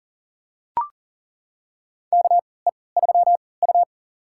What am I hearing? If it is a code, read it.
Morse code, heard as a keyed tone, about 40 wpm, spaced as for 23 wpm, 700 Hz. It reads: KE3U
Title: USA Call Signs - 3x Character Spacing - 40wpm